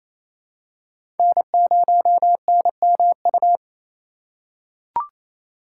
Morse code, heard as a keyed tone, about 28 wpm, 700 Hz. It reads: N0NMU
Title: USA Call Signs - 3x Character Spacing - 28wpm